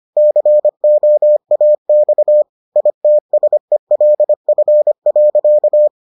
Morse code, heard as a keyed tone, about 25 wpm, 600 Hz. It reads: COAX ITSELF.